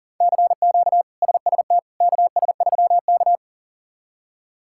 Morse code, heard as a keyed tone, about 37 wpm, 700 Hz. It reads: CQ SST KS3K
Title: K1USN SST QSO Practice - 40wpm